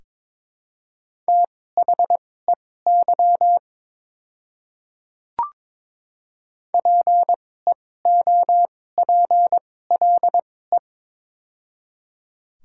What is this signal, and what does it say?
Morse code, heard as a keyed tone, about 22 wpm, spaced as for 16 wpm, 700 Hz. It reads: THEY PEOPLE
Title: 2x Character Spacing - Top 500 Words - 22wpm